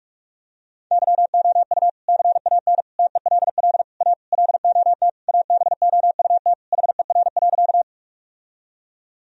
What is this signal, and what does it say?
Morse code, heard as a keyed tone, about 42 wpm, 700 Hz. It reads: YOU CAN TELL A LOT ABOUT HER.
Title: Sentences from Top 1000 Words - Review Part 3 - 45wpm